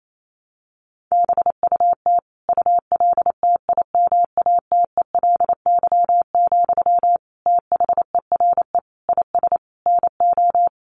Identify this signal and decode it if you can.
Morse code, heard as a keyed tone, about 28 wpm, 700 Hz. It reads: BUT ULTIMATELY, THERE IS NO
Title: Sentences from Top 1200 Words 28wpm